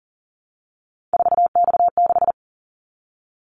Morse code, heard as a keyed tone, about 38 wpm, 700 Hz. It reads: 4X6
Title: Callsign Prefixes for Top 50 DXCC - 40wpm